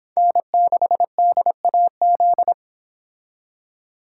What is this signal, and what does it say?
Morse code, heard as a keyed tone, about 26 wpm, 700 Hz. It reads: N6DAZ